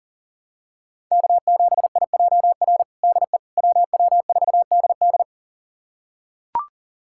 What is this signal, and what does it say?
Morse code, heard as a keyed tone, about 40 wpm, 700 Hz. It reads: K7IJR DE WW4DD
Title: USA Callsign DE Callsign - 5 Characters - 40wpm